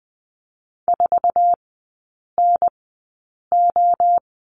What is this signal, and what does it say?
Morse code, heard as a keyed tone, about 20 wpm, spaced as for 8 wpm, 700 Hz. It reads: 4NO
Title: 053 Lesson 10e - Letter r - Three Characters 20wpm